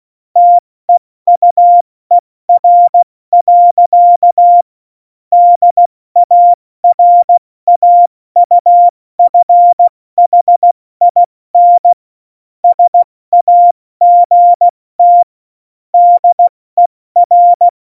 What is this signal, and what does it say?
Morse code, heard as a keyed tone, about 16 wpm, spaced as for 14 wpm, 700 Hz. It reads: TEUER. DARAUFHIN SAGT DER
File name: Morsekurs / Witze / Witz 23. April 2019, 70 F80